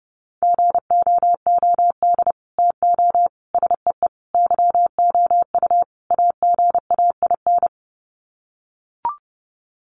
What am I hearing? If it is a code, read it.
Morse code, heard as a keyed tone, about 30 wpm, 700 Hz. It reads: GOOD TO SEE YOU AGAIN
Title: Sets of 5 Words Encore2 30wpm